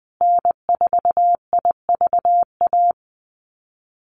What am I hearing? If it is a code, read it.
Morse code, heard as a keyed tone, about 20 wpm, 700 Hz. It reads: N4IVA
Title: USA Call Signs Repeated 2x 20wpm